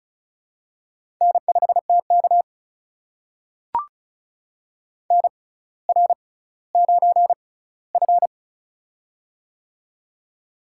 Morse code, heard as a keyed tone, about 36 wpm, spaced as for 18 wpm, 700 Hz. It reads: N5TK NR9F
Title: USA Call Signs - 4 Characters - 6x Character Spacing - 35wpm